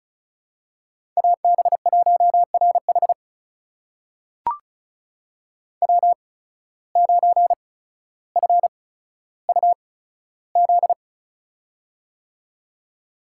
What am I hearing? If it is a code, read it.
Morse code, heard as a keyed tone, about 36 wpm, spaced as for 15 wpm, 700 Hz. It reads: AB1RH W9FUZ
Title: USA Call Signs - 5 Characters - 8x Character Spacing - 35wpm